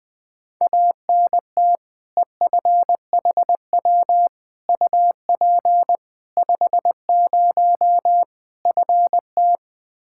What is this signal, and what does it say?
Morse code, heard as a keyed tone, about 20 wpm, 700 Hz. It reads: ANT EFHW UP 50 FT
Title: Practice QSO - Exchange 2 - Info - 20wpm